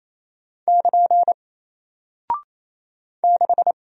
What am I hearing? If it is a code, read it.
Morse code, heard as a keyed tone, about 28 wpm, 700 Hz. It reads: <KN> 6
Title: Single Character (Letter:Number:Punctuation:Prosign) Alt 28wpm